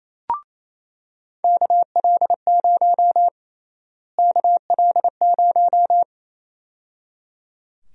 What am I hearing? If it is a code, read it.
Morse code, heard as a keyed tone, about 27 wpm, 700 Hz. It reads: KL0 KL0
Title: USA Call Sign Prefixes - Repeated 2x - 28wpm